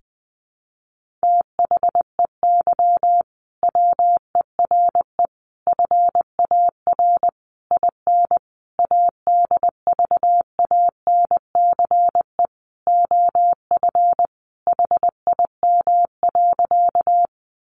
Morse code, heard as a keyed tone, about 20 wpm, 700 Hz. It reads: THEY WERE FAR IN ADVANCE OF HIM.
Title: Sentences from Top 1400 Words 20wpm